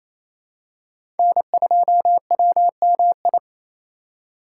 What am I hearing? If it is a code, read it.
Morse code, heard as a keyed tone, about 27 wpm, 700 Hz. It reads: N2WMI
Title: USA Call Signs - 5 Characters - 4x Character Spacing - 28wpm